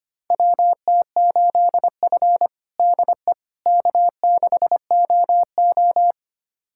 Morse code, heard as a keyed tone, about 25 wpm, 700 Hz. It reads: WT8F DE K6OO
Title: USA Callsign DE Callsign - 4 Characters - 25wpm